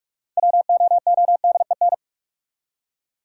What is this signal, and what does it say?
Morse code, heard as a keyed tone, about 45 wpm, 700 Hz. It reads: WOODEN
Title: Six Letter Words - 45wpm